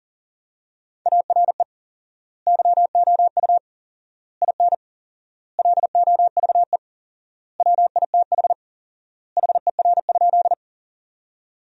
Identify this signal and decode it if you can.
Morse code, heard as a keyed tone, about 37 wpm, 700 Hz. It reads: ARE YOU IN LOVE WITH HER?
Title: 4x Word-Spacing - Sentences from Top 400 Words - 40wpm